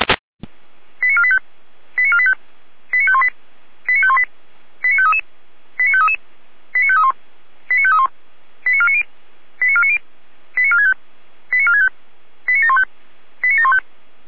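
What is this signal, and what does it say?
Five-tone selective call (ZVEI), heard as quick runs of five tones, tones of about 70 ms. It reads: 89485 89485 89518 89518 89533 89533 89531 89531 89490 89490 89465 89465 89715 89715
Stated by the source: ZVEI five-tone selective call received on SDR, code 89436.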